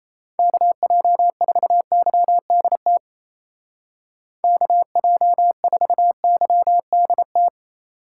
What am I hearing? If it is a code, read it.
Morse code, heard as a keyed tone, about 34 wpm, 700 Hz. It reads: KJ4YDT KJ4YDT
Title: Speed Racing - USA 2x3 Call Signs 28wpm